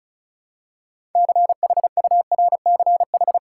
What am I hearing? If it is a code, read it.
Morse code, heard as a keyed tone, about 36 wpm, 700 Hz. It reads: CHURCH
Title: Six Letter Words - 35wpm